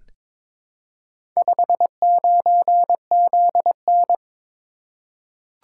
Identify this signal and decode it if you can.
Morse code, heard as a keyed tone, about 22 wpm, 700 Hz. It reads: 59ZN
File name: practice / 22wpm / Four Letter-Number Combinations - 22wpm